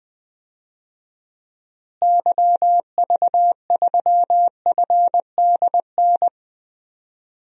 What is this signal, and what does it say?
Morse code, heard as a keyed tone, about 20 wpm, 700 Hz. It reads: YV3FDN
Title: DXCC-100 Call Signs - 6 Characters - Repeated 3x - 20wpm